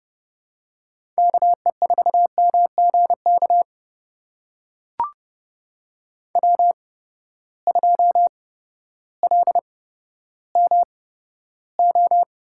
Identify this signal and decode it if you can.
Morse code, heard as a keyed tone, about 30 wpm, spaced as for 14 wpm, 700 Hz. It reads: KE4MGK W2LMO
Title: USA Call Signs - 8x Character Spacing - 30wpm